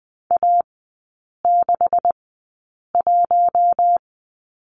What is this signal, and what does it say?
Morse code, heard as a keyed tone, about 20 wpm, spaced as for 8 wpm, 700 Hz. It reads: A61
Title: 152 Lesson 25e - Letter p - Three Characters 20wpm